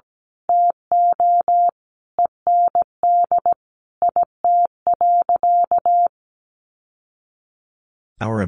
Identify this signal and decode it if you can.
Morse code, heard as a keyed tone, about 17 wpm, 700 Hz. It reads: TO END IT.
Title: Sentences from Top 1200 Words 17wpm